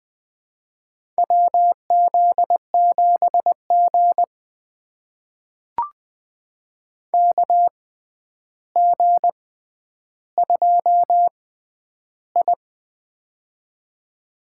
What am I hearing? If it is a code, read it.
Morse code, heard as a keyed tone, about 20 wpm, spaced as for 10 wpm, 700 Hz. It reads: WZ7G KG2I
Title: USA Call Signs - 4 Characters - 6x Character Spacing - 20wpm